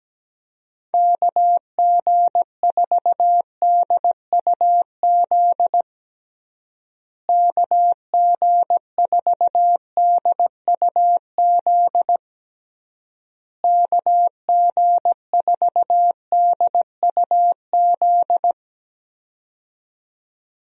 Morse code, heard as a keyed tone, about 17 wpm, 700 Hz. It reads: KG4DUZ KG4DUZ KG4DUZ
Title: USA Call Signs - 6 Characters - Repeated 3x - 17wpm